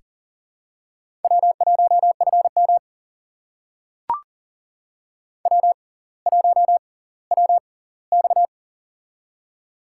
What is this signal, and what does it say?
Morse code, heard as a keyed tone, about 40 wpm, spaced as for 20 wpm, 700 Hz. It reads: W1FM W1WX
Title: USA Call Signs - 6x Character Spacing - 40wpm